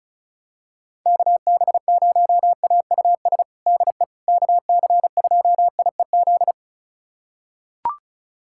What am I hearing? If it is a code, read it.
Morse code, heard as a keyed tone, about 34 wpm, 700 Hz. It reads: KB0AUS DE KC2IEZ
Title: USA Callsign DE Callsign 35wpm